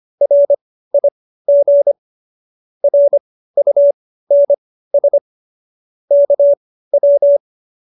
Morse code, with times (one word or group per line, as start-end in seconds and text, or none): 0.2-1.9 RIG
2.8-5.2 RUNS
6.1-7.4 KW